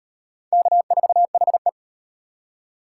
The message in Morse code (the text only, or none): K4HE